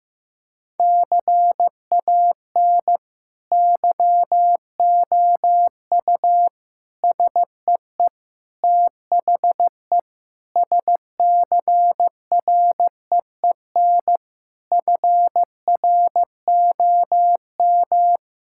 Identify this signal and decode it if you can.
Morse code, heard as a keyed tone, about 16 wpm, 700 Hz. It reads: CAN YOU SEE THE SCREEN FROM
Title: Sentences from Top 1000 Words - Review Part 6 - 15wpm